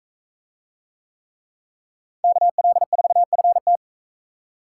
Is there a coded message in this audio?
Yes: Morse code from 2.2 to 3.8 s, heard as a keyed tone, about 44 wpm, 700 Hz. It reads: KL4FT